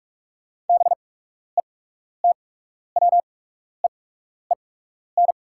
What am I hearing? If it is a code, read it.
Morse code, heard as a keyed tone, about 44 wpm, spaced as for 12 wpm, 690 Hz. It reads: BETWEEN
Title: 8x Character Spacing - Top 500 Words - 45wpm